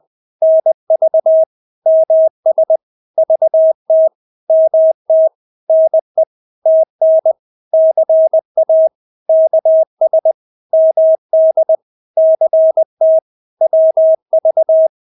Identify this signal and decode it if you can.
Morse code, heard as a keyed tone, about 20 wpm, 640 Hz. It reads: NV MS VT MT NE TN CA KS MD CT WV
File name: State-Abbreviations-20